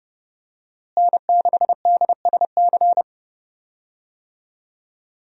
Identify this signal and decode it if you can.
Morse code, heard as a keyed tone, about 30 wpm, 700 Hz. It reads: N6DSC
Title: USA Call Signs - 5 Characters - Repeated 2x - 30wpm